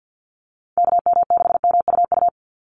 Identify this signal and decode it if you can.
Morse code, heard as a keyed tone, about 50 wpm, 700 Hz. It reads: KM6MUU
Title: USA Call Signs 50wpm